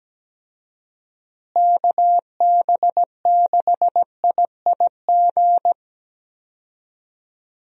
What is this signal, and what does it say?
Morse code, heard as a keyed tone, about 17 wpm, 700 Hz. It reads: KB6IIG
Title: USA Call Signs - 6 Characters - Repeated 2x - 17wpm